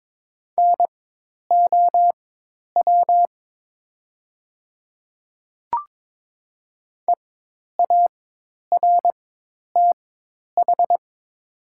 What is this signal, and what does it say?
Morse code, heard as a keyed tone, about 22 wpm, spaced as for 10 wpm, 700 Hz. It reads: NOW EARTH
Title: 4x Character Spacing - Top 300 Words - 22wpm